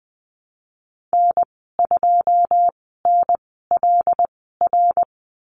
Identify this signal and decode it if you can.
Morse code, heard as a keyed tone, about 20 wpm, spaced as for 15 wpm, 700 Hz. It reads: N2NLR
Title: 088 Lesson 15g - Number 5 - Call Signs 20wpm